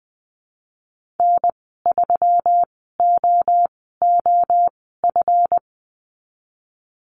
Morse code, heard as a keyed tone, about 20 wpm, spaced as for 14 wpm, 700 Hz. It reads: N3OOF
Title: 140 Lesson 23g - Letter f - Call Signs 20wpm